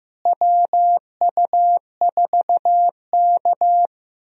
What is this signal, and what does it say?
Morse code, heard as a keyed tone, about 15 wpm, 700 Hz. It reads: WU4K
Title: USA Call Signs - 4 Characters - Repeated 3x - 15wpm